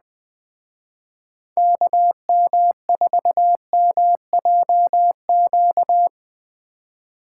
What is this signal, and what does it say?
Morse code, heard as a keyed tone, about 21 wpm, 700 Hz. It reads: KM4MJQ
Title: Speed Racing - USA 2x3 Call Signs 17wpm